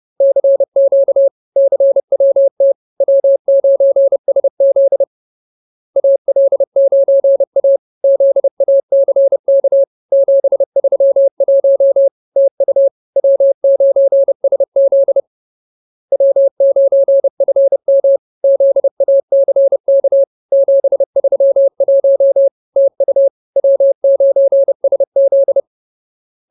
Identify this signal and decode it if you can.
Morse code, heard as a keyed tone, about 30 wpm, 550 Hz. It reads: CQ CWT W9SZ AL9A ZACK 731 TU W9SZ W9FM ZACK 731 TU W9SZ